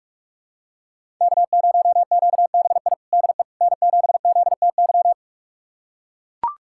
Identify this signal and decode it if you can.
Morse code, heard as a keyed tone, about 42 wpm, 700 Hz. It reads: K0QBI DE N7ZTY
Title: USA Callsign DE Callsign - 5 Characters - 45wpm